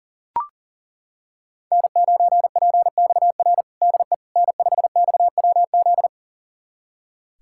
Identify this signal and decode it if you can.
Morse code, heard as a keyed tone, about 40 wpm, 700 Hz. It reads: N9PXR DE N5XWZ
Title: USA Callsign DE Callsign - 5 Characters - 40wpm